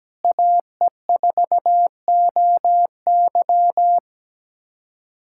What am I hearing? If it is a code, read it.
Morse code, heard as a keyed tone, about 17 wpm, 700 Hz. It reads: AE4OY
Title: USA Callsign DE Callsign 17wpm